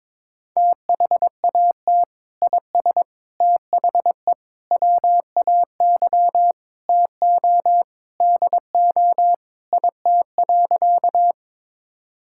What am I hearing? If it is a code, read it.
Morse code, heard as a keyed tone, about 22 wpm, 700 Hz. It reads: THAT IS THE WAY TO DO IT.
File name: practice / 22wpm / Sentences with 3-Word Phrases - 22wpm